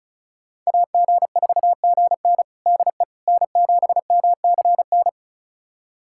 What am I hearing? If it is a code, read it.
Morse code, heard as a keyed tone, about 34 wpm, 700 Hz. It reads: AG4GN DE N7MCN
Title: USA Callsign DE Callsign - 5 Characters - 35wpm